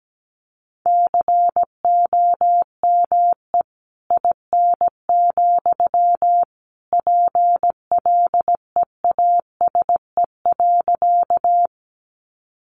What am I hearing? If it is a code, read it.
Morse code, heard as a keyed tone, about 17 wpm, 700 Hz. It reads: COME IN, PLEASE.
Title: Sentences from Top 1500 Words 17wpm